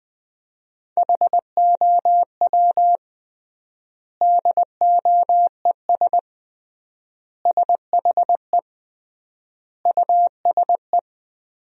Morse code, heard as a keyed tone, about 20 wpm, 700 Hz. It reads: HOW DOES SHE USE
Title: 3x Word-Spacing - Sentences from Top 200 Words - 20wpm